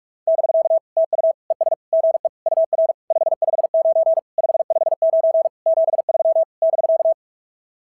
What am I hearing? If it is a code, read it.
Morse code, heard as a keyed tone, about 44 wpm, 650 Hz. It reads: <BK> TU ES GE UR 559 559 73 <BK>
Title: SOTA QSO Practice 45wpm